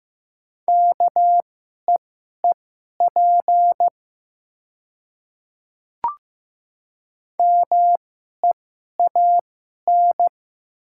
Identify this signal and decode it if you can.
Morse code, heard as a keyed tone, about 15 wpm, spaced as for 11 wpm, 700 Hz. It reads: KEEP MEAN
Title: 2x Character Spacing - Top 300 Words - 15wpm